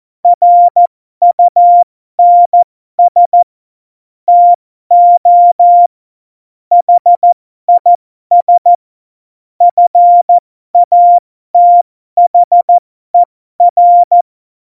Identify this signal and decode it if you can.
Morse code, heard as a keyed tone, about 14 wpm, spaced as for 12 wpm, 700 Hz. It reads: RUNS TO HIS FATHER
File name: Morsekurs / Witze